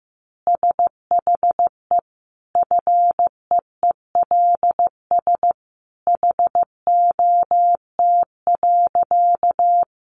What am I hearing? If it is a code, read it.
Morse code, heard as a keyed tone, about 15 wpm, 700 Hz. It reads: SHE FEELS HOT.
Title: Additional Easy Phrases 15wpm